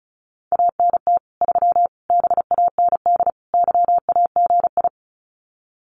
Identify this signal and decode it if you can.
Morse code, heard as a keyed tone, about 34 wpm, 700 Hz. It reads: ANT 3 BAND YAGI
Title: Common QSO Phrases 35wpm